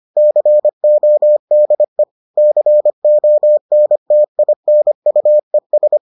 Morse code, heard as a keyed tone, about 25 wpm, 600 Hz. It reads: CODE CONTINUES